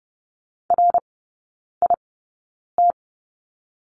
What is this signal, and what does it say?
Morse code, heard as a keyed tone, about 30 wpm, spaced as for 9 wpm, 700 Hz. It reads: RIT
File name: 081 Lesson 14e - Number 2 - Three Characters 30wpm